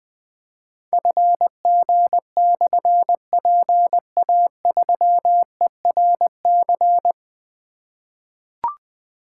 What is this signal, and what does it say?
Morse code, heard as a keyed tone, about 20 wpm, 700 Hz. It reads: FG/PA3ERC
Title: Difficult Call Signs 20wpm